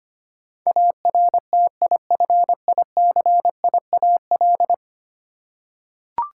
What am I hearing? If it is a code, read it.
Morse code, heard as a keyed tone, about 25 wpm, 700 Hz. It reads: ARTIFICIAL
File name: Ten Letter Words - 25wpm